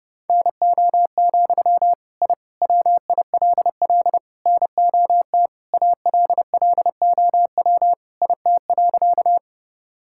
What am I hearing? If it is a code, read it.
Morse code, heard as a keyed tone, about 30 wpm, 700 Hz. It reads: NO, I WILL NOT ALLOW IT.
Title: Sentences from Top 1000 Words - Review Part 5 - 30wpm